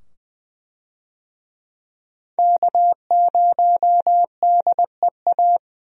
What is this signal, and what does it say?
Morse code, heard as a keyed tone, about 20 wpm, 700 Hz. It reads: K0DEA
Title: USA Call Signs - 5 Characters - 20wpm